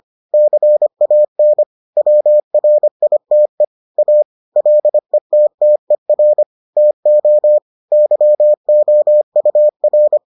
Morse code, heard as a keyed tone, about 25 wpm, 600 Hz. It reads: CAN WRITE A LETTER TO YOUR